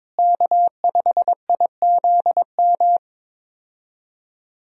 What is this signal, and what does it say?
Morse code, heard as a keyed tone, about 22 wpm, 700 Hz. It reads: K5IZM
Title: USA Call Signs - 5 Characters - Repeated 3x - 22wpm